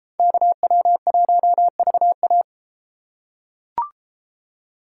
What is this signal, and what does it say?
Morse code, heard as a keyed tone, about 31 wpm, 700 Hz. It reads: KW1VA